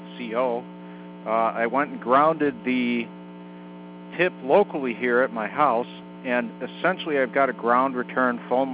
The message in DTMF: none